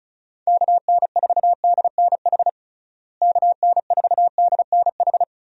Morse code, heard as a keyed tone, about 34 wpm, 700 Hz. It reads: KN4DNH KN4DNH
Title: USA Call Signs - 6 Characters - Repeated 3x - 35wpm